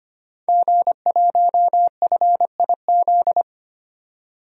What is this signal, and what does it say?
Morse code, heard as a keyed tone, about 25 wpm, 700 Hz. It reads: G1FIZ
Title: DXCC-100 Call Signs - 5 Characters - 25wpm